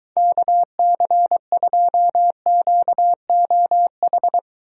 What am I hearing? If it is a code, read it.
Morse code, heard as a keyed tone, about 23 wpm, 700 Hz. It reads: KC2QOH